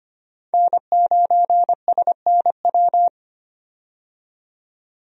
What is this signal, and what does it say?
Morse code, heard as a keyed tone, about 25 wpm, 690 Hz. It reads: N9SNW
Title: USA Call Signs - 5 Characters - 25wpm